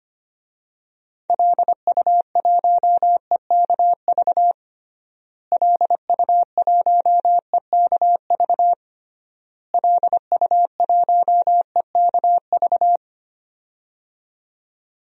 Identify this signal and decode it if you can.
Morse code, heard as a keyed tone, about 25 wpm, 700 Hz. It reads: LU1EKV LU1EKV LU1EKV
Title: DXCC-100 Call Signs - 6 Characters - Repeated 3x - 25wpm